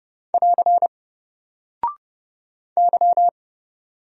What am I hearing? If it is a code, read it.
Morse code, heard as a keyed tone, about 30 wpm, 700 Hz. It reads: <AR> Y
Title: Single Character (Letter:Number:Punctuation:Prosign) Alt 30wpm